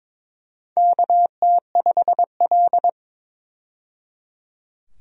Morse code, heard as a keyed tone, about 22 wpm, 700 Hz. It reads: KT5L
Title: USA Call Signs - 4 Characters - 22wpm